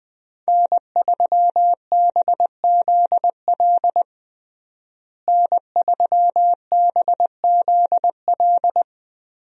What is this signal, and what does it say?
Morse code, heard as a keyed tone, about 20 wpm, 700 Hz. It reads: N3BZL N3BZL
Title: USA Call Signs - 5 Characters - Repeated 3x - 20wpm